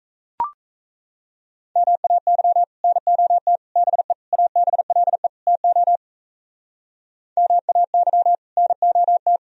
Morse code, heard as a keyed tone, about 35 wpm, 700 Hz. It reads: MAY NOT BE ABLE TO MAY NOT
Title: Speed-Racing - Sets of 5 Words 28wpm